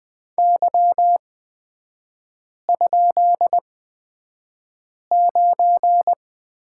Morse code, heard as a keyed tone, about 20 wpm, spaced as for 5 wpm, 690 Hz. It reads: Y?9
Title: 229 Lesson 37e - Letter x - Three Characters 20wpm